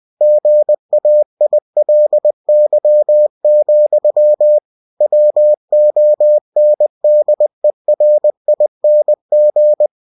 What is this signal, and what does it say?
Morse code, heard as a keyed tone, about 20 wpm, 600 Hz. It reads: GAILY, WONDERING
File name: cw-books / winnie-the-pooh / winnie-20wpm-page-0013